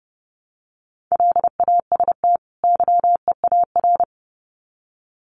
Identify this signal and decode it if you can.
Morse code, heard as a keyed tone, about 29 wpm, 700 Hz. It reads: LAST YEAR
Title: Sets of 2 Words 30wpm